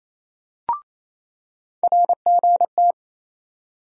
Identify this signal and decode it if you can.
Morse code, heard as a keyed tone, about 28 wpm, 700 Hz. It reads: RGT
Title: USA Call Sign Postfixes - 3 Letters 28wpm